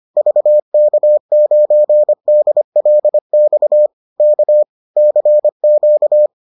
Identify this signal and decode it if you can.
Morse code, heard as a keyed tone, about 25 wpm, 600 Hz. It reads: VK9DLX K CQ